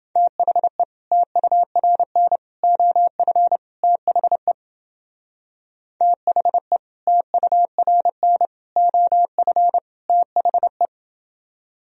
Morse code, heard as a keyed tone, about 31 wpm, 700 Hz. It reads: THE TURN OF THE THE TURN OF THE